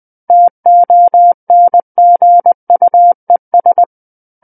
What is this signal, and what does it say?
Morse code, heard as a keyed tone, about 20 wpm, 700 Hz. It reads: TONGUES
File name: 20 WPM